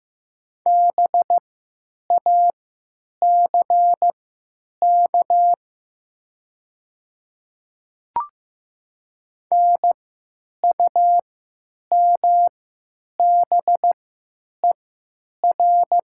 Morse code, heard as a keyed tone, about 15 wpm, spaced as for 9 wpm, 700 Hz. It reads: BACK NUMBER
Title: 3x Character Spacing - Top 200 Words - 15wpm